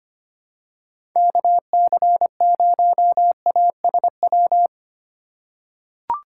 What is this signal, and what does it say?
Morse code, heard as a keyed tone, about 25 wpm, 700 Hz. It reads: KC0ASW
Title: USA Call Signs - 6 Characters - 2x Character Spacing - 25wpm